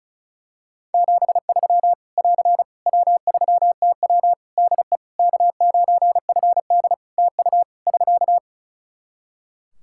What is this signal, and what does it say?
Morse code, heard as a keyed tone, about 34 wpm, 700 Hz. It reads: 73 <AR> W3TW DE K9FD TU <SK>
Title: Practice QSO - Exchange 4 - Ending - 35wpm